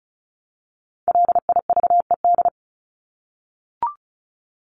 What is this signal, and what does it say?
Morse code, heard as a keyed tone, about 34 wpm, 700 Hz. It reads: LIVED
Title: ED Suffix - Top 100 Words with Base Words 35wpm